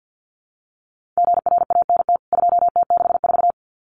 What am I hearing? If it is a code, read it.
Morse code, heard as a keyed tone, about 50 wpm, 700 Hz. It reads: GRANT 2T64
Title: CWT Exchange 50wpm